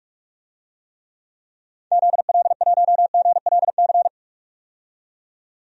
Morse code, heard as a keyed tone, about 46 wpm, 700 Hz. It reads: ZL1GLC
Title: DXCC-100 Call Signs - 6 Characters - 45wpm